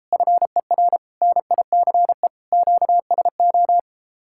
Morse code, heard as a keyed tone, about 33 wpm, 700 Hz. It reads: FER NICE QSO